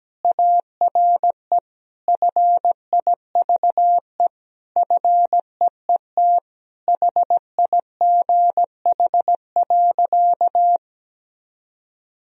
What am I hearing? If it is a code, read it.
Morse code, heard as a keyed tone, about 17 wpm, 700 Hz. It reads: ARE FIVE FEET HIGH.
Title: Sentences from Top 1000 Words - Review Part 2 - 17wpm